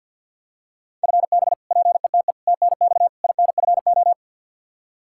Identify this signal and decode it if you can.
Morse code, heard as a keyed tone, about 47 wpm, 700 Hz. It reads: FB PETE TNX INFO
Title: Practice QSO - Exchange 3 - More Info - 50wpm